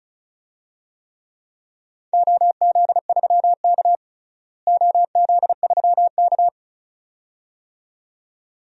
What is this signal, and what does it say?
Morse code, heard as a keyed tone, about 36 wpm, 700 Hz. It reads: OZ3K OZ3K
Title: DXCC-100 Call Signs - 4 Characters - Repeated 2x - 35wpm